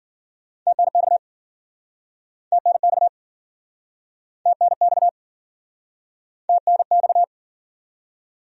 Morse code, heard as a keyed tone, about 51 wpm, 700 Hz. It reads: TNX TNX TNX TNX